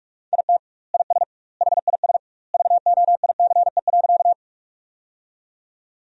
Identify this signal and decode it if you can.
Morse code, heard as a keyed tone, about 43 wpm, 700 Hz. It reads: IT IS HIS VOICE.
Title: 2x Word-Spacing - Sentences from Top 500 Words - 45wpm